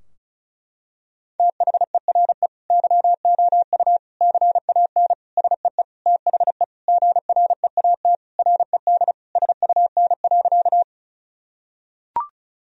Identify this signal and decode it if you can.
Morse code, heard as a keyed tone, about 34 wpm, 700 Hz. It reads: THERE YOU CAN SEE THE GREAT RED SUN.
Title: Sentences from Top 1000 Words - Review Part 2 - 35wpm